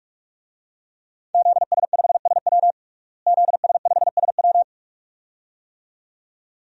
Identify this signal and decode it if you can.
Morse code, heard as a keyed tone, about 46 wpm, 700 Hz. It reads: ZS5SW ZS5SW
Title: DXCC-100 Call Signs - 5 Characters - Repeated 2x - 45wpm